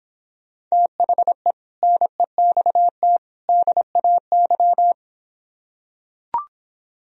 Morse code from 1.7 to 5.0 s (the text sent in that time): NEXT DAY